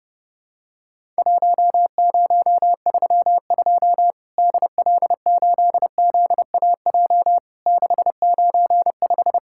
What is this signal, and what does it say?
Morse code, heard as a keyed tone, about 29 wpm, 700 Hz. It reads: 1032 DL8ZAJ 695
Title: Worked All Europe DX Contest - QTC Exchange - 30wpm